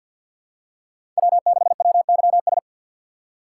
Morse code, heard as a keyed tone, about 52 wpm, 700 Hz. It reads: W6WYS
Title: USA Call Signs - 3x Character Spacing - 50wpm